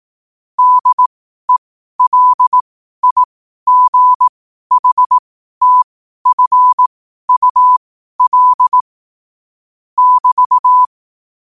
Morse code, heard as a keyed tone, about 18 wpm, spaced as for 12 wpm, 1000 Hz. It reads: DELIGHTFUL <BT>
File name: 12wpm